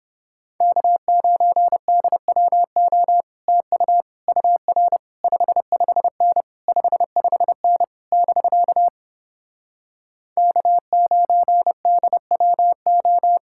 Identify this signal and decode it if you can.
Morse code, heard as a keyed tone, about 29 wpm, 700 Hz. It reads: K9DWO TU UR 55N 55N <BK> K9DWO